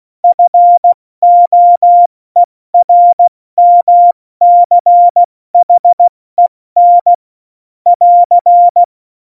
Morse code, heard as a keyed tone, about 16 wpm, spaced as for 14 wpm, 700 Hz. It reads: FOERMCHEN <AR>